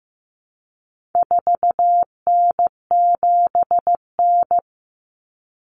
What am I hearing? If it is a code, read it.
Morse code, heard as a keyed tone, about 15 wpm, 700 Hz. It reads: 4N7N
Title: Common Cut Numbers - 4-Digit 15wpm